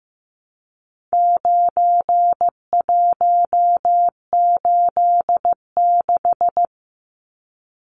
Morse code, heard as a keyed tone, about 15 wpm, 700 Hz. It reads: 9186
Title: Numbers - 4-Digit 15wpm